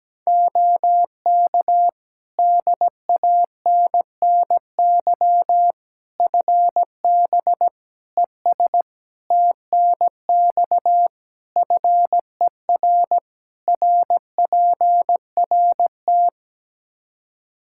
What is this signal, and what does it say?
Morse code, heard as a keyed tone, about 17 wpm, 700 Hz. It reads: OK DANNY FB ES TNX FER RPRT
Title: Practice QSO - Exchange 2 - Info - 17wpm